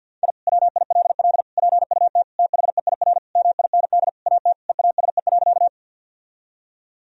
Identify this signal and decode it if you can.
Morse code, heard as a keyed tone, about 51 wpm, 700 Hz. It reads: I WILL PUT THEIR MIND AT EASE.
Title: Sentences from Top 1000 Words - Review Part 1 - 50wpm